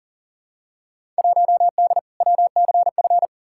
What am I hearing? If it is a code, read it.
Morse code, heard as a keyed tone, about 40 wpm, 700 Hz. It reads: 1D WCF
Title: Field Day Exchanges - 40wpm